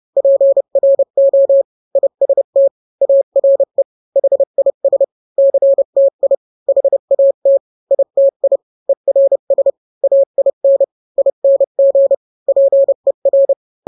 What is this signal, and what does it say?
Morse code, heard as a keyed tone, about 30 wpm, 550 Hz. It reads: PRO IST ARE HIS CTI HAT ITI ERS AIN ING PER